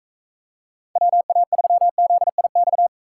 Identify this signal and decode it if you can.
Morse code, heard as a keyed tone, about 40 wpm, 700 Hz. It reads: WA3ZIX